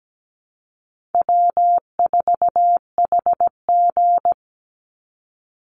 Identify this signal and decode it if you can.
Morse code, heard as a keyed tone, about 17 wpm, 700 Hz. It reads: W4HG
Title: Easy Call Signs Repeated 2x - 17wpm